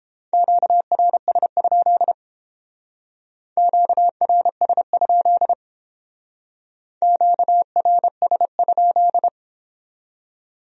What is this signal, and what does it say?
Morse code, heard as a keyed tone, about 29 wpm, 700 Hz. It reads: QRS? QRS? QRS?